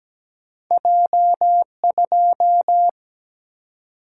Morse code, heard as a keyed tone, about 17 wpm, 700 Hz. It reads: J2
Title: Single-Letter Single-Number 17wpm